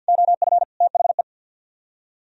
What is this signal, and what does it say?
Morse code, heard as a keyed tone, about 48 wpm, 700 Hz. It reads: OF THE
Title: Speed-Racing - Sets of 3 Words 50wpm